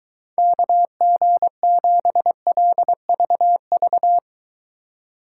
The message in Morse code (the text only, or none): KG7LVV